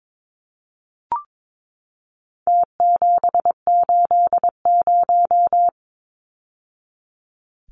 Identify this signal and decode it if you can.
Morse code, heard as a keyed tone, about 22 wpm, 700 Hz. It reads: T780
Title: Numbers - Contest Style - 4-Digit 22wpm